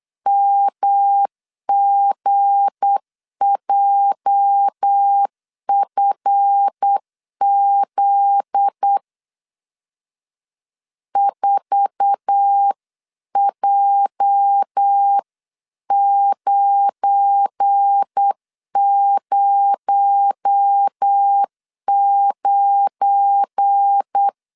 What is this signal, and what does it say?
Morse code, heard as a keyed tone, about 8 wpm, 800 Hz. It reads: MGJFZ 4J909
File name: CW / DET / 1 9 1500C S C 8wpm 5653